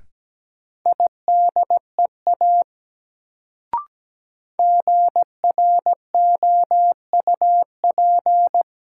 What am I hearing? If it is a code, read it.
Morse code, heard as a keyed tone, about 17 wpm, 700 Hz. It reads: IDEA GROUP
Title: Sets of 3 Words - New Word Drill - 17wpm